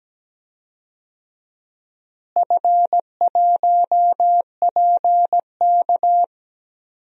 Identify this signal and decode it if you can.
Morse code, heard as a keyed tone, about 17 wpm, 700 Hz. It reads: F1PK
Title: DXCC-100 Call Signs - 4 Characters - Repeated 3x - 17wpm